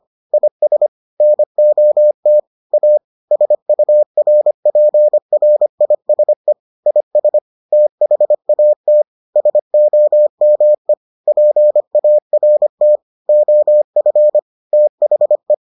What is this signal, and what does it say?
Morse code, heard as a keyed tone, about 25 wpm, 600 Hz. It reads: IS NOT A SURPRISE IS THAT SOME PART OF THE